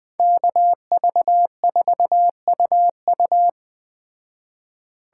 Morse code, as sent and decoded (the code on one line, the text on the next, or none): -.- ...- ....- ..- ..-
KV4UU